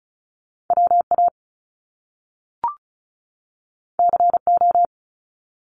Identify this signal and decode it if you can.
Morse code, heard as a keyed tone, about 35 wpm, 700 Hz. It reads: WA CO
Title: US States 35wpm